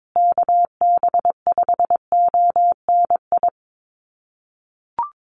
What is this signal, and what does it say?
Morse code, heard as a keyed tone, about 22 wpm, 700 Hz. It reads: KB5ONI
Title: USA Call Signs Repeated 3x 22wpm